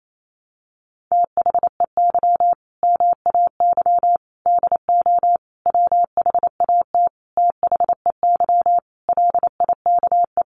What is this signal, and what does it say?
Morse code, heard as a keyed tone, about 28 wpm, 700 Hz. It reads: THEY MAY DO WHAT THEY LIKE
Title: Sentences from Top 1200 Words 28wpm